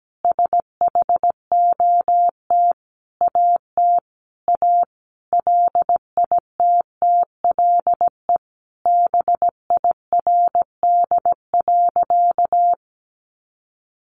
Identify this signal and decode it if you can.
Morse code, heard as a keyed tone, about 17 wpm, 700 Hz. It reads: SHOT AT A LITTLE BIRD.
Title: Sentences from Top 1100 Words 17wpm